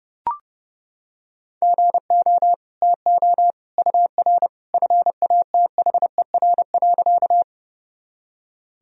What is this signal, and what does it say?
Morse code, heard as a keyed tone, about 32 wpm, 700 Hz. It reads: GO TO UR FATHER.
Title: Sentences from Top 300 Words and Abbreviations 30wpm